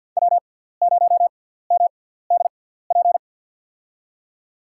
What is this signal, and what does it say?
Morse code, heard as a keyed tone, about 48 wpm, spaced as for 17 wpm, 700 Hz. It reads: W0MDP
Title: USA Call Signs - 5 Characters - 6x Character Spacing - 50wpm